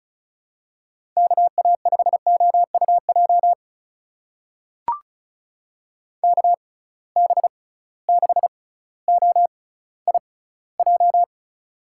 Morse code, heard as a keyed tone, about 36 wpm, spaced as for 18 wpm, 690 Hz. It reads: KA5OUJ KB6OIJ
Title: USA Call Signs - 6 Characters - 6x Character Spacing - 35wpm